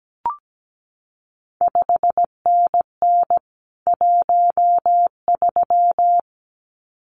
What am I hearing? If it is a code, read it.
Morse code, heard as a keyed tone, about 17 wpm, 700 Hz. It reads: 5NN 13
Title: CQ WW DX Contest Exchange Encore - 17wpm